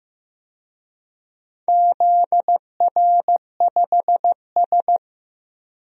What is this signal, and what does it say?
Morse code, heard as a keyed tone, about 15 wpm, 700 Hz. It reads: ZR5S
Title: DXCC-100 Call Signs - 4 Characters - 15wpm